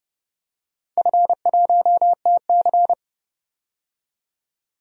Morse code, heard as a keyed tone, about 31 wpm, 700 Hz. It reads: F1TC
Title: DXCC-100 Call Signs - 4 Characters - 30wpm